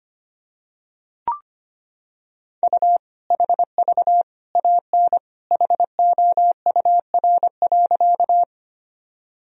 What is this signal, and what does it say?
Morse code, heard as a keyed tone, about 25 wpm, 700 Hz. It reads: U HV AN HOUR.
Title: Sentences from Top 400 Words and Abbreviations 25wpm